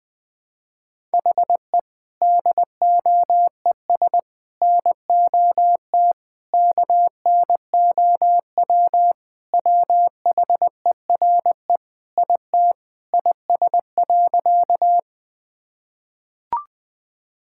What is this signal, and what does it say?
Morse code, heard as a keyed tone, about 20 wpm, 700 Hz. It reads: HE DOES NOT KNOW WHERE IT IS.
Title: Sentences with 4-Word Phrases - Spoken Beginning - 20wpm